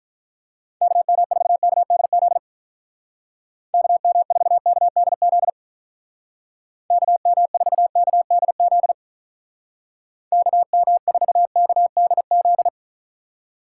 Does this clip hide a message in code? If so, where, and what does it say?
Yes: Morse code from 0.8 to 12.7 s, heard as a keyed tone, about 44 wpm, 700 Hz. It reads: KM4KDZ KM4KDZ KM4KDZ KM4KDZ